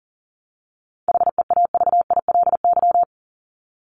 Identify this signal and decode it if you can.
Morse code, heard as a keyed tone, about 40 wpm, 700 Hz. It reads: HEAVILY
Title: LY Suffix - Top 100 Words with Base Words 40wpm